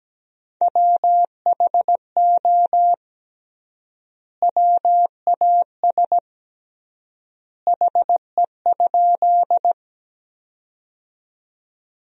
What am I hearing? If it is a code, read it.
Morse code, heard as a keyed tone, about 17 wpm, 700 Hz. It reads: WHO WAS HE?
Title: 3x Word-Spacing - Sentences from Top 100 Words - 17wpm